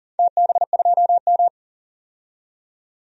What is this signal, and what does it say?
Morse code, heard as a keyed tone, about 40 wpm, 700 Hz. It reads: TB2M
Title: DXCC-100 Call Signs - 4 Characters - 40wpm